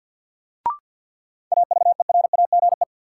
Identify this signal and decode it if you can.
Morse code, heard as a keyed tone, about 52 wpm, 700 Hz. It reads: AVERAGE
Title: Sentences from Top 1400 Words - New Word Drill Alt - 50wpm